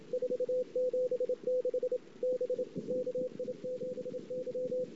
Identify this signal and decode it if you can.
Morse code, heard as a keyed tone, about 27 wpm, 500 Hz. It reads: 476 B KI7Y
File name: ss-cw